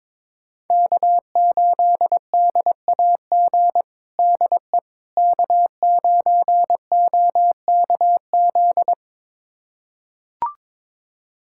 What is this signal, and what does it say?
Morse code, heard as a keyed tone, about 21 wpm, 700 Hz. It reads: K8DAG DE K9OKZ
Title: USA Callsign DE Callsign - 5 Characters - 22wpm